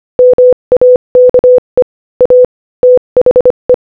Morse code, heard as a keyed tone, about 25 wpm, 500 Hz. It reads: MAKE A THE
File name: Morse-Code-Audio-Speed-25-4